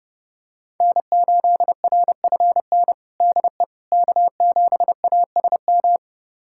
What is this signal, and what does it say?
Morse code, heard as a keyed tone, about 30 wpm, 700 Hz. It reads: N8RFN DE K7ASM
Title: USA Callsign DE Callsign - 5 Characters - 30wpm